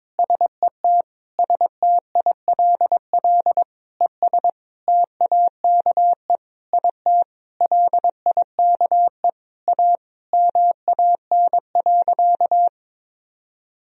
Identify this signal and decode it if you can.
Morse code, heard as a keyed tone, about 22 wpm, 700 Hz. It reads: SET STILL ES TAKE IT LIKE A MAN.
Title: Sentences from Top 300 Words and Abbreviations 22wpm